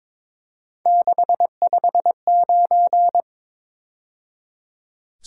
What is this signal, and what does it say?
Morse code, heard as a keyed tone, about 27 wpm, 700 Hz. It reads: 659
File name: Speed-Racing - Three Letter-Number Combinations - 22wpm